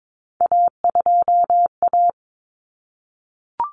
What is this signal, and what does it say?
Morse code, heard as a keyed tone, about 22 wpm, 700 Hz. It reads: A2A
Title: Common Cut Numbers - 3-Digit 22wpm